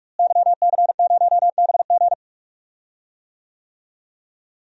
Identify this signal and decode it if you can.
Morse code, heard as a keyed tone, about 44 wpm, 700 Hz. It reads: YC0BG
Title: DXCC-100 Call Signs - 5 Characters - Repeated 2x - 45wpm